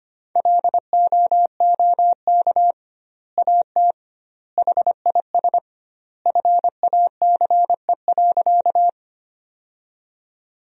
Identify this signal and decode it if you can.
Morse code, heard as a keyed tone, about 25 wpm, 700 Hz. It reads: LOOK AT HIS FACE.
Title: 2x Word-Spacing - Sentences from Top 300 Words - 25wpm